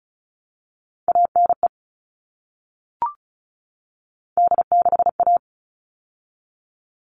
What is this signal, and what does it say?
Morse code, heard as a keyed tone, about 34 wpm, 700 Hz. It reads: ANE D6A
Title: All Cut Numbers - 3-Digit 35wpm